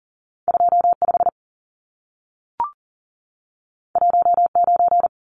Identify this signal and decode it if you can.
Morse code, heard as a keyed tone, about 40 wpm, 700 Hz. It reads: 25 19
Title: Numbers - 2-Digit 40wpm